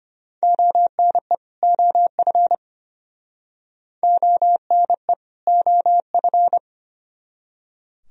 Morse code, heard as a keyed tone, about 34 wpm, 700 Hz. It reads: ONE OF ONE OF